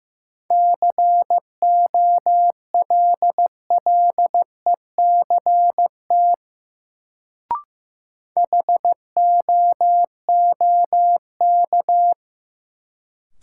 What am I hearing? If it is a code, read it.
Morse code, heard as a keyed tone, about 15 wpm, 700 Hz. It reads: COLLECT HOOK
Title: Sentences from Top 1600 Words - New Word Drill - 15wpm